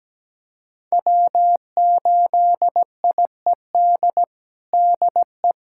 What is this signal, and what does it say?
Morse code, heard as a keyed tone, about 17 wpm, 700 Hz. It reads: W8IED DE
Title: USA Callsign DE Callsign - 5 Characters - 17wpm